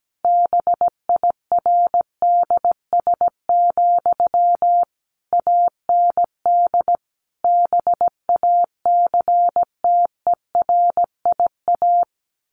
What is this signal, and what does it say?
Morse code, heard as a keyed tone, about 17 wpm, 700 Hz. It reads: BIRDS, AND BACTERIA
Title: A Sound Of Thunder 17wpm